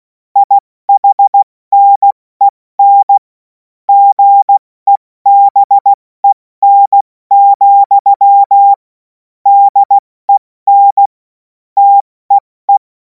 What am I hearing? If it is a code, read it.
Morse code, heard as a keyed tone, about 16 wpm, 800 Hz. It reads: IHNEN GEBEN, DEN TEE